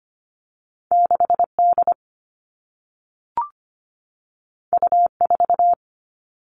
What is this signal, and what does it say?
Morse code, heard as a keyed tone, about 25 wpm, 700 Hz. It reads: 6D U4
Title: All Cut Numbers - 2-Digit 25wpm